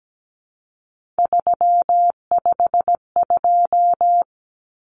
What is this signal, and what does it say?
Morse code, heard as a keyed tone, about 17 wpm, 700 Hz. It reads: 352
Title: Numbers - 3-Digit 17wpm